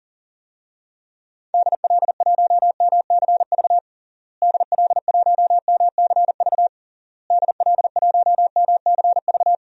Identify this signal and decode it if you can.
Morse code, heard as a keyed tone, about 40 wpm, 700 Hz. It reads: DL1MCV DL1MCV DL1MCV
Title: DXCC-100 Call Signs - 6 Characters - Repeated 3x - 40wpm